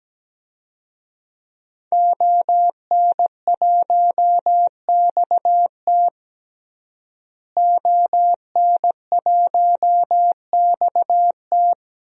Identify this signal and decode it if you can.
Morse code, heard as a keyed tone, about 17 wpm, 700 Hz. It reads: ON1XT ON1XT
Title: DXCC-100 Call Signs - 5 Characters - Repeated 2x - 17wpm